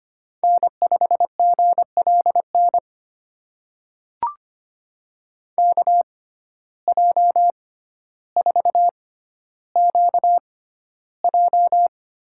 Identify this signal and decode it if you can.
Morse code, heard as a keyed tone, about 25 wpm, spaced as for 13 wpm, 700 Hz. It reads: N5GLN KJ4QJ
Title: USA Call Signs - 5 Characters - 6x Character Spacing - 25wpm